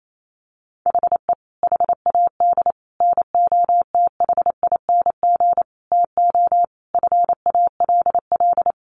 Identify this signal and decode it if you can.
Morse code, heard as a keyed tone, about 28 wpm, 700 Hz. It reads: HE HAD NOTHING TO FALL
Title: Sentences from Top 1100 Words 28wpm